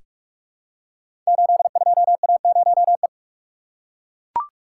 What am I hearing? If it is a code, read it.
Morse code, heard as a keyed tone, about 42 wpm, 700 Hz. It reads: 82A0E
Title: Code Groups - 5 Letter:Numbers - 45wpm